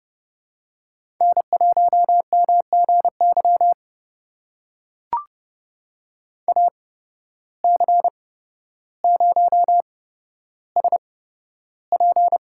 Morse code, heard as a keyed tone, about 30 wpm, spaced as for 13 wpm, 700 Hz. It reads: N1MGY AC0SP
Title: USA Call Signs - 8x Character Spacing - 30wpm